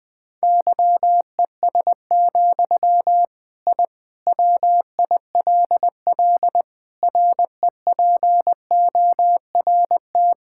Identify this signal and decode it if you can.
Morse code, heard as a keyed tone, about 19 wpm, 700 Hz. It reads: YES, I WILL REPORT
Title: Sentences from Top 1000 Words - Review Part 6 - 20wpm